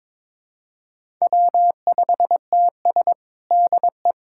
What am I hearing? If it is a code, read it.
Morse code, heard as a keyed tone, about 22 wpm, 700 Hz. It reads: W5TS DE
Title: USA Callsign DE Callsign - 4 Characters - 22wpm